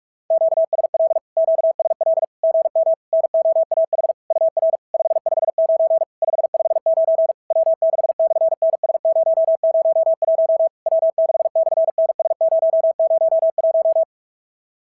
Morse code, heard as a keyed tone, about 44 wpm, 660 Hz. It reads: QSL QSL GM NOAH UR 559 559 W6/NS001 W6/NS001
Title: SOTA QSO Practice 45wpm